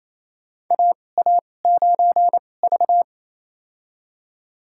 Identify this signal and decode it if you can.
Morse code, heard as a keyed tone, about 28 wpm, spaced as for 20 wpm, 700 Hz. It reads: AA9V
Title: USA Call Signs - 4 Characters - 2x Character Spacing - 28wpm